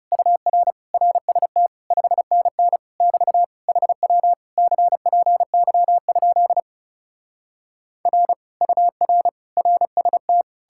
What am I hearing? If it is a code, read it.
Morse code, heard as a keyed tone, about 40 wpm, 700 Hz. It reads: UR RST 5NN <BT> HW CPY? R UR RST